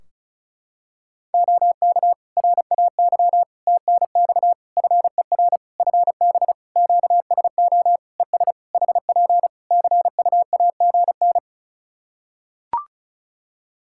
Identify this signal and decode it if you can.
Morse code, heard as a keyed tone, about 34 wpm, 700 Hz. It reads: OK RAY TNX FER FB QSO ES HP CUAGN
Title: Practice QSO - Exchange 4 - Ending - 35wpm